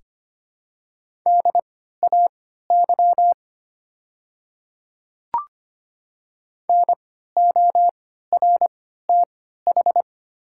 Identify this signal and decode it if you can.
Morse code, heard as a keyed tone, about 25 wpm, spaced as for 14 wpm, 700 Hz. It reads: DAY NORTH
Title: 3x Character Spacing - Top 300 Words - 25wpm